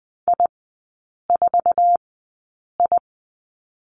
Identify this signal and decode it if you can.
Morse code, heard as a keyed tone, about 20 wpm, spaced as for 8 wpm, 700 Hz. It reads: I4I
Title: 060 Lesson 11e - Letter h - Three Characters 20wpm